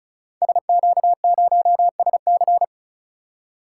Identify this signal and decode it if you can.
Morse code, heard as a keyed tone, about 36 wpm, 700 Hz. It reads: SQ0SC
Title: Code Groups - 5 Letter:Numbers - 35wpm